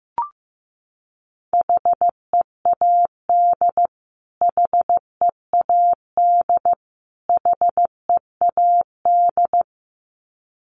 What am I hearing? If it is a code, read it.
Morse code, heard as a keyed tone, about 15 wpm, 700 Hz. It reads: HEAD HEAD HEAD
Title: Top 500 Words Repeated 15wpm